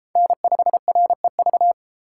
Morse code, heard as a keyed tone, about 30 wpm, 700 Hz. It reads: N5REV